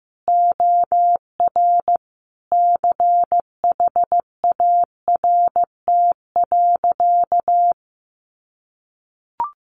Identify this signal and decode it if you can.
Morse code, heard as a keyed tone, about 15 wpm, 700 Hz. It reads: OR CHART.
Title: Sentences from Top 1300 Words 15wpm